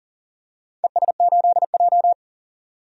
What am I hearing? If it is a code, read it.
Morse code, heard as a keyed tone, about 40 wpm, 700 Hz. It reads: ES8J
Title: DXCC-100 Call Signs - 4 Characters - 40wpm